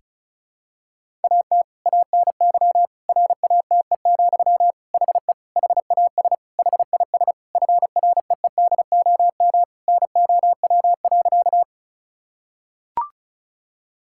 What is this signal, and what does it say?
Morse code, heard as a keyed tone, about 34 wpm, 700 Hz. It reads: AT ANY RATE, HE HAS HIS FREEDOM NOW.
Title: Sentences from Top 1000 Words - Review Part 5 - 35wpm